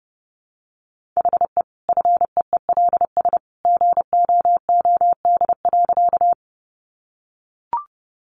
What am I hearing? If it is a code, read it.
Morse code, heard as a keyed tone, about 30 wpm, 700 Hz. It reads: HE FEELS GOOD.
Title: Additional Easy Phrases 30wpm